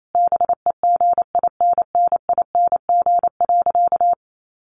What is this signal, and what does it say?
Morse code, heard as a keyed tone, about 28 wpm, 700 Hz. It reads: BEGINNING.
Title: Sentences from Top 800 Words 28wpm